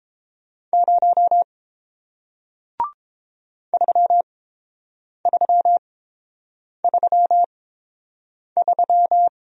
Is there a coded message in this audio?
Yes: Morse code from 0.7 to 9.3 s, heard as a keyed tone, about 30 wpm, spaced as for 7 wpm, 700 Hz. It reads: 0 3333